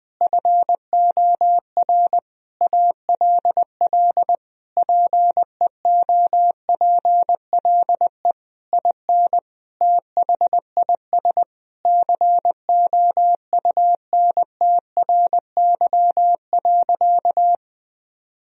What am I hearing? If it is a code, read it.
Morse code, heard as a keyed tone, about 20 wpm, 700 Hz. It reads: FOR ALL PEOPLE IN THIS COUNTRY.
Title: Sentences from Top 1000 Words - Review Part 3 - 20wpm